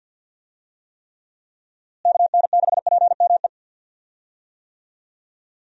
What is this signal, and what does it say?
Morse code, heard as a keyed tone, about 51 wpm, 700 Hz. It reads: KN6PME
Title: USA Call Signs - 6 Characters - Repeated 2x - 50wpm